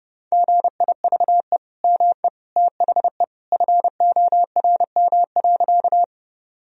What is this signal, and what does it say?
Morse code, heard as a keyed tone, about 30 wpm, 700 Hz. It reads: GIVE ME THE FORM.
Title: Sentences from Top 1000 Words - Review Part 1 - 30wpm